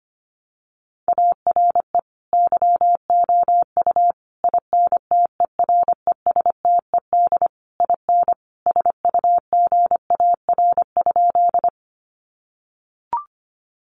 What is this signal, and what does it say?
Morse code, heard as a keyed tone, about 25 wpm, 700 Hz. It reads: ARE YOU INTERESTED IN SUGAR?
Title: Sentences from Top 1200 Words 25wpm